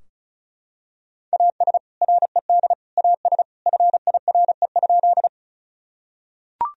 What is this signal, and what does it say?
Morse code, heard as a keyed tone, about 36 wpm, 700 Hz. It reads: AS RED AS FIRE?
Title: Sentences from Top 1000 Words - Review Part 2 - 35wpm